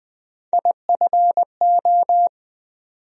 Morse code, heard as a keyed tone, about 20 wpm, 700 Hz. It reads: IFO
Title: USA Call Sign Postfixes - 3 Letters 20wpm